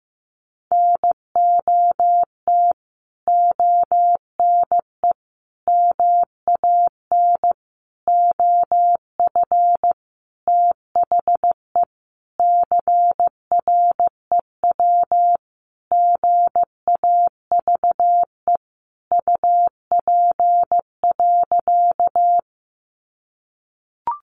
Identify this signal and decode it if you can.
Morse code, heard as a keyed tone, about 15 wpm, 700 Hz. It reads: NOT ONE MAN OF THE CREW GAVE UP.
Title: Sentences from Top 1500 Words 15wpm